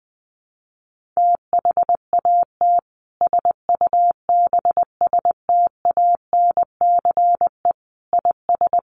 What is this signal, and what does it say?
Morse code, heard as a keyed tone, about 20 wpm, 700 Hz. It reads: THAT SUBSTANCE IS
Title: Sentences from Top 1500 Words 20wpm